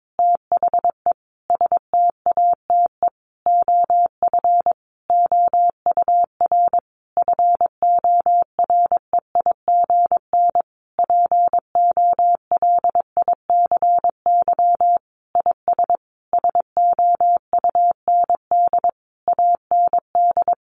Morse code, heard as a keyed tone, about 22 wpm, 700 Hz. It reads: THE STATE OF OUR FOREIGN POLICY IS SOUND AND
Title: Sentences from Top 1400 Words 22wpm